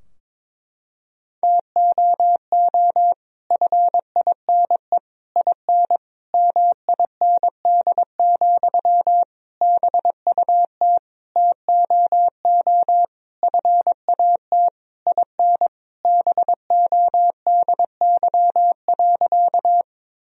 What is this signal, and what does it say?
Morse code, heard as a keyed tone, about 22 wpm, 700 Hz. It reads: TOO FINE IN MIND, BUT TOO FAT IN BODY.
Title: Sentences from Top 1000 Words - Review Part 4 - 22wpm